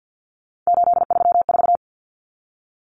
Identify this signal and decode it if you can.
Morse code, heard as a keyed tone, about 50 wpm, 700 Hz. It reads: 834
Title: Common Cut Numbers - 3-Digit 50wpm